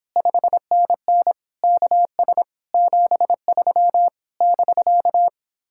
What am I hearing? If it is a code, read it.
Morse code, heard as a keyed tone, about 29 wpm, 700 Hz. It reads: 5NN KS 73 <BK>